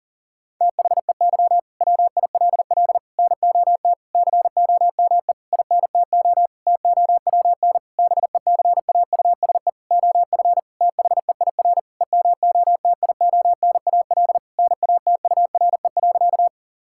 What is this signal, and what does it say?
Morse code, heard as a keyed tone, about 40 wpm, 700 Hz. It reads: THEY WILL NOT COME INTO TOWN BECAUSE OF THEIR EMOTIONAL NATURE.
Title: Sentences from Top 1000 Words - Review Part 5 - 40wpm